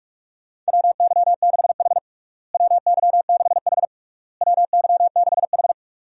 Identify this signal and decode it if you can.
Morse code, heard as a keyed tone, about 46 wpm, 700 Hz. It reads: WY6H WY6H WY6H
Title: USA Call Signs - 4 Characters - Repeated 3x - 45wpm